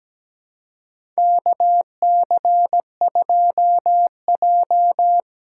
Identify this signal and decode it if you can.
Morse code, heard as a keyed tone, about 17 wpm, 700 Hz. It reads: KC2J
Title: USA Call Signs - 4 Characters - 4x Character Spacing - 17wpm